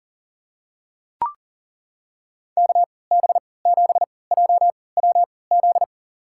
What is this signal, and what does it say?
Morse code, heard as a keyed tone, about 40 wpm, spaced as for 23 wpm, 700 Hz. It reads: KB7JWZ
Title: USA Call Signs - 6 Characters - 3x Character Spacing - 40wpm